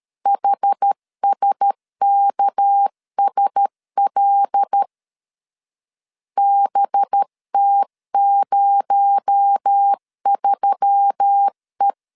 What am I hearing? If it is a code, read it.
Morse code, heard as a keyed tone, about 13 wpm, 800 Hz. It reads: HSKSL BT03E
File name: LEZIONI / 7 GF9Z / 7 GF9Z 12wpm 2147